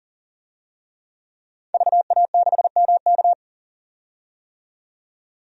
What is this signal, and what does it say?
Morse code, heard as a keyed tone, about 40 wpm, 700 Hz. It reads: VA6MK
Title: DXCC-100 Call Signs - 5 Characters - 40wpm